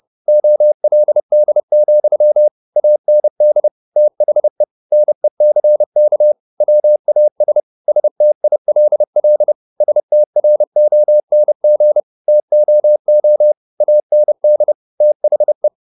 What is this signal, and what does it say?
Morse code, heard as a keyed tone, about 30 wpm, 600 Hz. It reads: OLD, AND THE NECK WAS STILL STRONG TOO AND THE
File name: the old man and the sea / the old man and the sea 1 30wpm